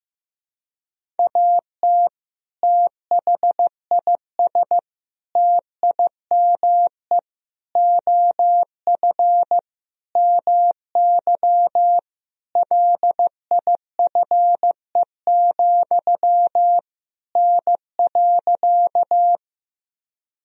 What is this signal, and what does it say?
Morse code, heard as a keyed tone, about 15 wpm, 700 Hz. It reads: AT THIS TIME OF MY LIFE, N.
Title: Sentences from Top 300 Words and Abbreviations 15wpm